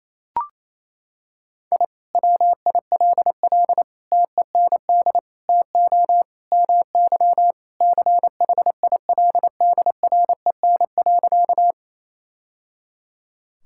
Morse code, heard as a keyed tone, about 28 wpm, 700 Hz. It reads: I WILL TEND TO MY CHILDREN.
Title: Sentences from Top 1000 Words - Review Part 5 - 28wpm